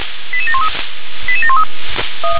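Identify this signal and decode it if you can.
Five-tone selective call (ZVEI), heard as quick runs of five tones, tones of about 70 ms. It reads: 88713 88713
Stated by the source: ZVEI five-tone call 88709 received by SDR.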